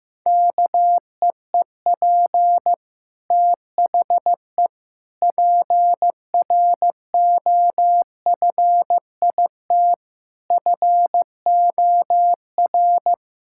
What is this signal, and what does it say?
Morse code, heard as a keyed tone, about 15 wpm, 700 Hz. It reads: KEEP THE PROFIT FOR
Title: Sentences from Top 1000 Words - Review Part 5 - 15wpm